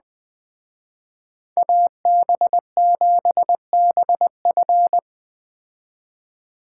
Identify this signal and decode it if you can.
Morse code, heard as a keyed tone, about 20 wpm, 700 Hz. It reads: AB7BF